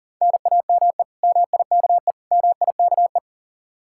Morse code, heard as a keyed tone, about 50 wpm, 700 Hz. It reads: NAME MIKE MIKE
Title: Speed-Racing Common QSO Phrases - 40wpm